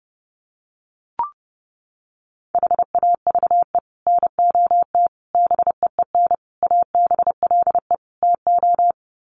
Morse code, heard as a keyed tone, about 30 wpm, 700 Hz. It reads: HAVE NOT BEEN ABLE TO
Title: Sets of 5 Words 30wpm